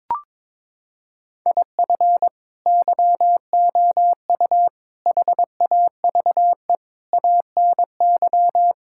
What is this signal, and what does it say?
Morse code, heard as a keyed tone, about 22 wpm, 700 Hz. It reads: IF YOU HAVE ANY
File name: practice / 22wpm / Sets of 8 Words - 22wpm